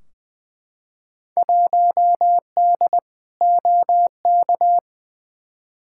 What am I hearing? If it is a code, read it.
Morse code, heard as a keyed tone, about 21 wpm, 700 Hz. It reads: 1D OK
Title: Field Day Exchanges - 20wpm